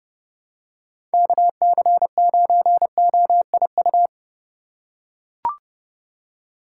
Morse code, heard as a keyed tone, about 31 wpm, 700 Hz. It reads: KC9OIU
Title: USA Call Signs - 6 Characters - 3x Character Spacing - 30wpm